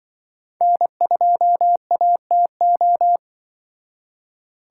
Morse code, heard as a keyed tone, about 24 wpm, 700 Hz. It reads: N2ATO
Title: Speed Racing - USA 1x3 Call Signs 20wpm